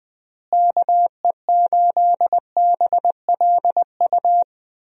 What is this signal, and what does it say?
Morse code, heard as a keyed tone, about 20 wpm, 700 Hz. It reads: KE8BLU
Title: USA Call Signs - 6 Characters - Repeated 3x - 20wpm